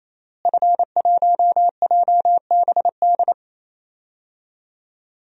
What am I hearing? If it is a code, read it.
Morse code, heard as a keyed tone, about 27 wpm, 700 Hz. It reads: F1JBD
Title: DXCC-100 Call Signs - 5 Characters - Repeated 3x - 28wpm